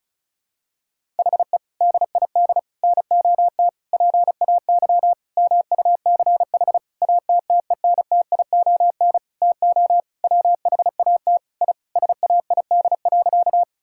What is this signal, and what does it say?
Morse code, heard as a keyed tone, about 36 wpm, 700 Hz. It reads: HE DID NOT PAY MUCH ATTENTION TO WHAT I SAID.
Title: Sentences from Top 1000 Words - Review Part 6 - 35wpm